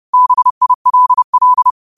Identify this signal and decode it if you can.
Morse code, heard as a keyed tone, about 30 wpm, 1000 Hz. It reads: BILL
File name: qotdfiles / 30wpm